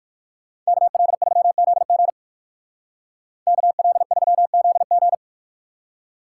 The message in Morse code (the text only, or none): KL3ZG KL3ZG